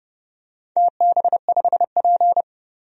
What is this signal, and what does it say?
Morse code, heard as a keyed tone, about 29 wpm, 700 Hz. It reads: TB5P
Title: DXCC-100 Call Signs - 4 Characters - 30wpm